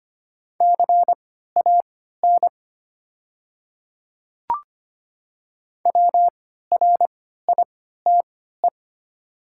Morse code, heard as a keyed tone, about 25 wpm, spaced as for 14 wpm, 700 Hz. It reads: CAN WRITE
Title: 3x Character Spacing - Top 100 Words - 25wpm